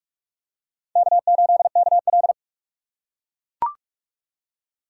Morse code, heard as a keyed tone, about 41 wpm, 700 Hz. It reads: K8KL